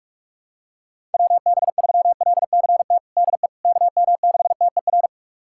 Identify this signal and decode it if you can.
Morse code, heard as a keyed tone, about 45 wpm, 700 Hz. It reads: WB3LCT DE KM6TER
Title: USA Callsign DE Callsign 45wpm